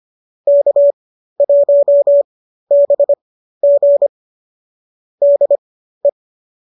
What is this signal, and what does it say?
Morse code, heard as a keyed tone, about 25 wpm, spaced as for 13 wpm, 570 Hz. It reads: K1BG DE